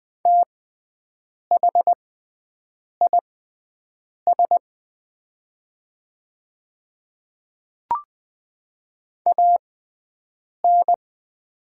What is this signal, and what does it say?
Morse code, heard as a keyed tone, about 20 wpm, spaced as for 7 wpm, 700 Hz. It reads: THIS AN
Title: 6x Character Spacing - Top 200 Words - 20wpm